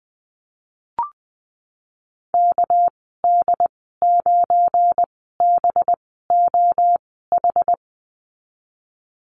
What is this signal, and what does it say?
Morse code, heard as a keyed tone, about 20 wpm, 700 Hz. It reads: K D 9 B O H
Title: 238 Lesson 38g - Letter q - Call Signs 20wpm